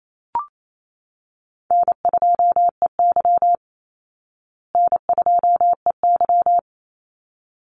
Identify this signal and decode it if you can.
Morse code, heard as a keyed tone, about 28 wpm, 700 Hz. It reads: N2EY N2EY
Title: Call Signs Repeated 3x - 28wpm